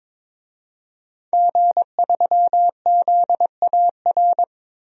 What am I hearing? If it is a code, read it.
Morse code, heard as a keyed tone, about 22 wpm, 700 Hz. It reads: G3ZAR
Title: DXCC-100 Call Signs - 5 Characters - 22wpm